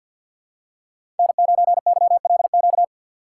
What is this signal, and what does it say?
Morse code, heard as a keyed tone, about 52 wpm, 700 Hz. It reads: N9YLQ
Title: USA Call Signs - 5 Characters - 2x Character Spacing - 50wpm